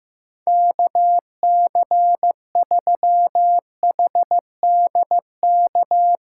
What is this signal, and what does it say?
Morse code, heard as a keyed tone, about 16 wpm, 700 Hz. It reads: KC3HDK
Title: USA Call Signs - 6 Characters - 2x Character Spacing - 15wpm